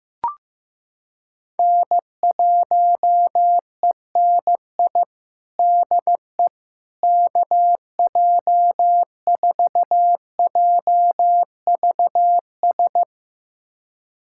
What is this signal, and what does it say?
Morse code, heard as a keyed tone, about 16 wpm, 700 Hz. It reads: N1ENI DE KJ4JVS
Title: USA Callsign DE Callsign 15wpm